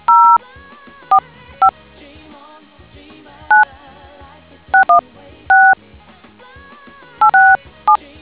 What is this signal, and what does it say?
DTMF, heard as telephone touch tones, tones of about 120 ms. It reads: 045 9 616 867